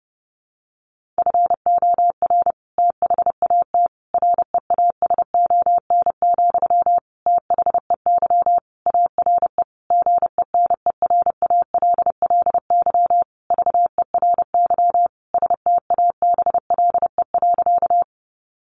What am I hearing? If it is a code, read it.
Morse code, heard as a keyed tone, about 30 wpm, 700 Hz. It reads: FOR THAT REASON, THEY ARE GENERALLY VERY STABLE.
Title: Sentences from Top 1500 Words 30wpm